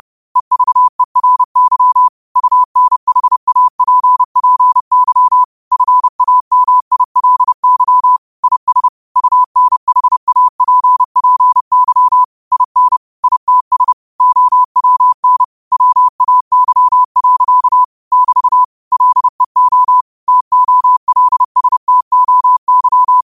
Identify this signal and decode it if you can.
Morse code, heard as a keyed tone, about 30 wpm, 1000 Hz. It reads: EVERY UNHAPPY FAMILY IS UNHAPPY IN ITS OWN WAY. <BT> LEO TOLSTOY
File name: 30wpm